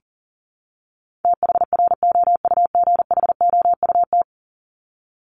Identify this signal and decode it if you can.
Morse code, heard as a keyed tone, about 40 wpm, 700 Hz. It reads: THROUGHOUT
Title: Top 751-1000 Words 40wpm